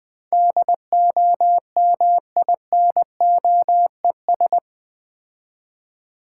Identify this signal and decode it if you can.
Morse code, heard as a keyed tone, about 24 wpm, 700 Hz. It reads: DOMINOES
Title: Easy Words - Speed-Racing with Hints - 20wpm